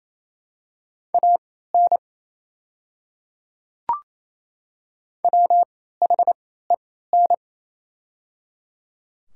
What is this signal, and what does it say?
Morse code, heard as a keyed tone, about 28 wpm, spaced as for 16 wpm, 700 Hz. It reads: AN WHEN
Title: 3x Character Spacing - Top 300 Words - 28wpm